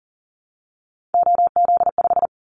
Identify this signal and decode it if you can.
Morse code, heard as a keyed tone, about 40 wpm, 700 Hz. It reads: OZ5
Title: Callsign Prefixes for Top 100 DXCC - 40wpm